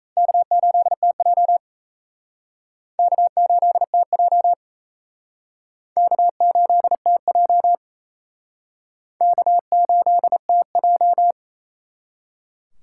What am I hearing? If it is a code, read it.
Morse code, heard as a keyed tone, about 34 wpm, 700 Hz. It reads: K8TJ K8TJ K8TJ K8TJ
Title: Speed Racing - USA 1x2 Call Signs 28wpm